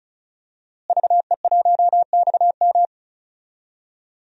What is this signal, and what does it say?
Morse code, heard as a keyed tone, about 36 wpm, 700 Hz. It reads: VE1XM
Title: DXCC-100 Call Signs - 5 Characters - Repeated 2x - 35wpm